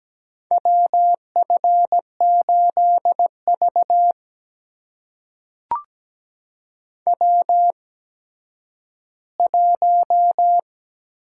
Morse code, heard as a keyed tone, about 17 wpm, spaced as for 10 wpm, 700 Hz. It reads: WF8V W1
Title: USA Call Signs - 4 Characters - 8x Character Spacing - 17wpm